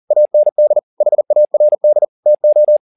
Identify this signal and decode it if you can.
Morse code, heard as a keyed tone, about 40 wpm, 600 Hz. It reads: AND HARD TO
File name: the old man and the sea 2 40wpm